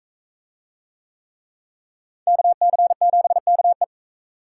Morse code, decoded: KC7KE

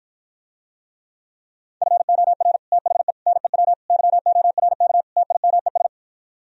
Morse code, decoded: FOR THE NEW YORK TIMES

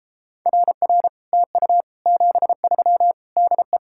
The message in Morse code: RR TU 73 DE

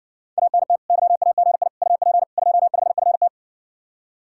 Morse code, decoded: ANT YAGI UP 25FT

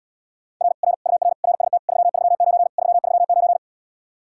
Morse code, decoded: R R UR RST 339 339

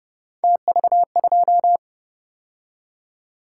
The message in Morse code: TV2

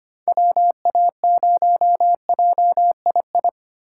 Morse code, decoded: WA0JII

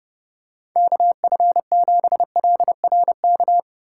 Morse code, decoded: KF7LRK

